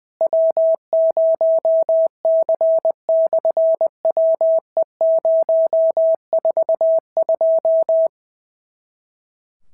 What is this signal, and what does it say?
Morse code, heard as a keyed tone, about 20 wpm, 650 Hz. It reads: W0C/WE042
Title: SOTA QSO Practice 20wpm